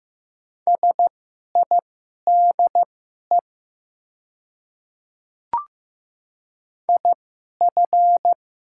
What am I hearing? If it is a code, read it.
Morse code, heard as a keyed tone, about 15 wpm, spaced as for 11 wpm, 700 Hz. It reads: SIDE IF
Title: 2x Character Spacing - Top 100 Words - 15wpm